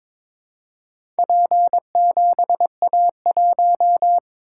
Morse code, heard as a keyed tone, about 22 wpm, 700 Hz. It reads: P7A1
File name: practice / 22wpm / Four Letter-Number Combinations - 22wpm